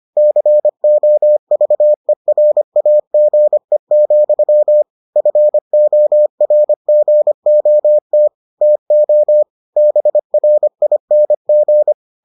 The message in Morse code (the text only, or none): COVERAGE, FORGOT TO BRING